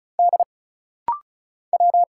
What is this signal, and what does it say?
Morse code, heard as a keyed tone, about 34 wpm, 700 Hz. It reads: D W
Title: Single Letter-Number Alt - Rapid-Fire 35wpm